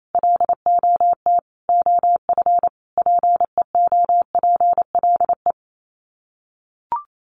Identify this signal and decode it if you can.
Morse code, heard as a keyed tone, about 28 wpm, 700 Hz. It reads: LOT OF PEOPLE
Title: Sets of 5 Words 28wpm